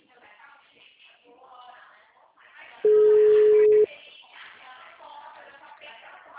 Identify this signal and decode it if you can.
Ringback tone (Europe, 425 Hz), with speech.